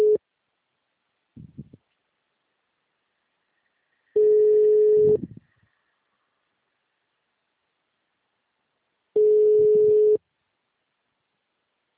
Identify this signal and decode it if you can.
Ringback tone (Europe, 425 Hz).
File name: out- -2002- - - .238